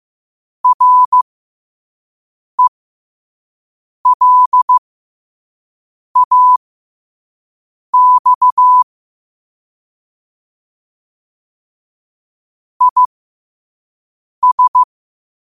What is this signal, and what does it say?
Morse code, heard as a keyed tone, about 15 wpm, spaced as for 5 wpm, 1000 Hz. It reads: RELAX IS
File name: qotdfiles / 5wpm